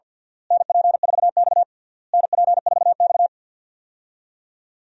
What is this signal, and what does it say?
Morse code, heard as a keyed tone, about 53 wpm, 700 Hz. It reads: NP4X NP4X
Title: USA Call Signs - 4 Characters - Repeated 3x - 50wpm